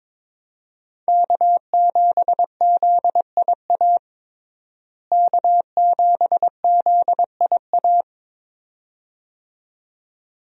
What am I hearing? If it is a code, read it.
Morse code, heard as a keyed tone, about 22 wpm, 700 Hz. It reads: K7ZIA K7ZIA
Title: USA Call Signs - 5 Characters - Repeated 2x - 22wpm